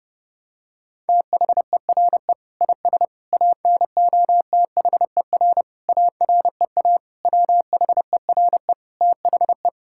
Morse code, heard as a keyed tone, about 32 wpm, 700 Hz. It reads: THERE IS ANOTHER AREA WHERE THE
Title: Sentences from Top 1000 Words - Review Part 5 - 30wpm